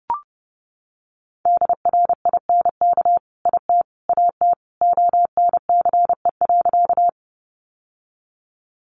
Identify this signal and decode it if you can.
Morse code, heard as a keyed tone, about 30 wpm, 700 Hz. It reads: DRINK IT AT ONCE.
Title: Sentences from Top 1500 Words 30wpm